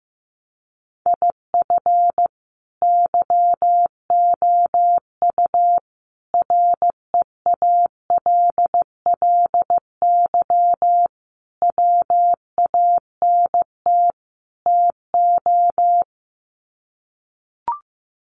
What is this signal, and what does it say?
Morse code, heard as a keyed tone, about 15 wpm, 700 Hz. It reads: IF YOU REALLY WANT TO
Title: Sets of 5 Words Encore2 15wpm